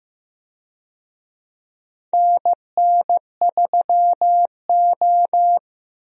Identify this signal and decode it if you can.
Morse code, heard as a keyed tone, about 15 wpm, 700 Hz. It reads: NN3O
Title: USA Call Signs - 4 Characters - Repeated 3x - 15wpm